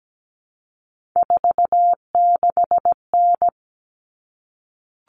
Morse code, heard as a keyed tone, about 17 wpm, 700 Hz. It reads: 46N
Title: Common Cut Numbers - 3-Digit 17wpm